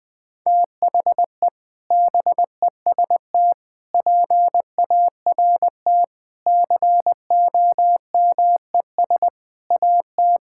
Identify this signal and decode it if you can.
Morse code, heard as a keyed tone, about 20 wpm, 700 Hz. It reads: THE BEST PART COMES AT
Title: Sentences with 6-Word Phrases - 20wpm